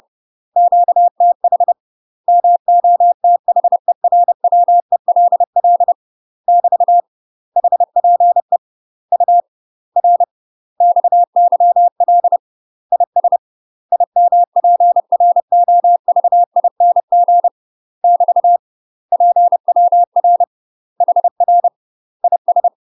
Morse code, heard as a keyed tone, about 30 wpm, 700 Hz. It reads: QTH MOTHERWELL <BT> HPE U R XYL IS IMPROVING <BT> PWR HR IS
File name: UK MASTER QSOs TRAINING / 30WPM / UK MASTER QSOs 0040